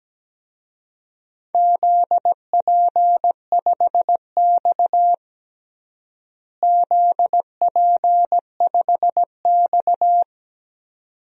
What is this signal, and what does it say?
Morse code, heard as a keyed tone, about 17 wpm, 700 Hz. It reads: ZP5X ZP5X
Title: DXCC-100 Call Signs - 4 Characters - Repeated 2x - 17wpm